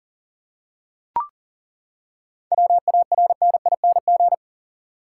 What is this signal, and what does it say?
Morse code, heard as a keyed tone, about 40 wpm, 700 Hz. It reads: WARNING
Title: Seven Letter Words - 40wpm